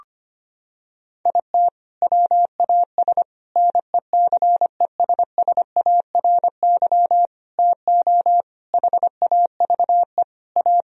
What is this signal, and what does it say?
Morse code, heard as a keyed tone, about 25 wpm, 700 Hz. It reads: IT WAS NECESSARY TO HAVE A
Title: Sentences from Top 1000 Words - Review Part 1 - 25wpm